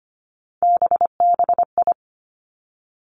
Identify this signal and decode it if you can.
Morse code, heard as a keyed tone, about 25 wpm, 700 Hz. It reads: BBI
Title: Three Challenging Letter-Number Combinations 25wpm